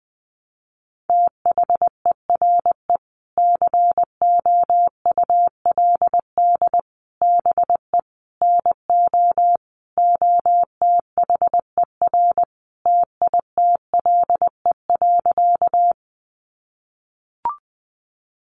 Morse code, heard as a keyed tone, about 20 wpm, 700 Hz. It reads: THERE COULD BE NO OTHER TITLE.
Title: Sentences from Top 1200 Words 20wpm